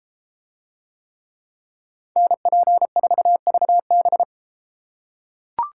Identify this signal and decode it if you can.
Morse code, heard as a keyed tone, about 32 wpm, 700 Hz. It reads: NP4VB